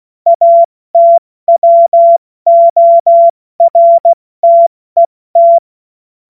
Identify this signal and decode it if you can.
Morse code, heard as a keyed tone, about 16 wpm, 680 Hz. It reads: ATWORTET
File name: Morsekurs / Witze / Witz 13. Januar 2026 70 F80